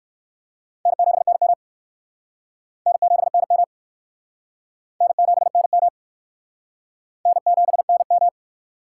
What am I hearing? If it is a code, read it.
Morse code, heard as a keyed tone, about 59 wpm, 700 Hz. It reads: N7NM N7NM N7NM N7NM